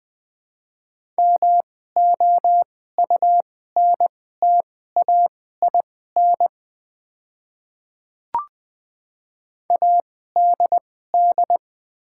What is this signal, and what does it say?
Morse code, heard as a keyed tone, about 20 wpm, spaced as for 14 wpm, 700 Hz. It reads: MOUNTAIN ADD
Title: 2x Character Spacing - Top 300 Words - 20wpm